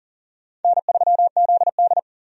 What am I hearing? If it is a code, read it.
Morse code, heard as a keyed tone, about 40 wpm, 700 Hz. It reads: N3ZD
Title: USA Call Signs - 4 Characters - 40wpm